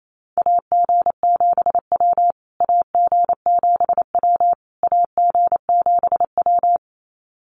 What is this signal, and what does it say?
Morse code, heard as a keyed tone, about 28 wpm, 700 Hz. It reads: AG7W AG7W AG7W
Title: USA Call Signs Repeated 3x 28wpm